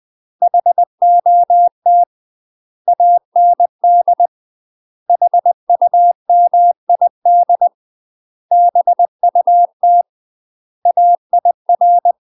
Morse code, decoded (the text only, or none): HOT AND HUMID BUT AIR